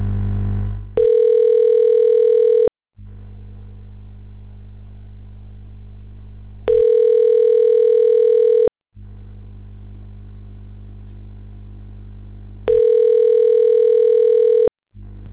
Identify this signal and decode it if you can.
Ringback tone (North America), 440 + 480 Hz.